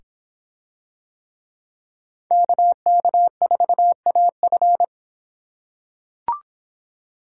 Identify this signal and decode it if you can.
Morse code, heard as a keyed tone, about 26 wpm, 700 Hz. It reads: KK4AF